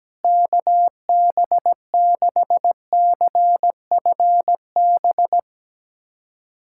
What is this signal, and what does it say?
Morse code, heard as a keyed tone, about 17 wpm, 700 Hz. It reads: KB6CFB
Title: USA Callsign DE Callsign 17wpm